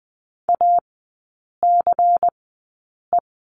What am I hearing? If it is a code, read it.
Morse code, heard as a keyed tone, about 20 wpm, spaced as for 8 wpm, 700 Hz. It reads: ACE
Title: 093 Lesson 16e - Letter c - Three Characters 20wpm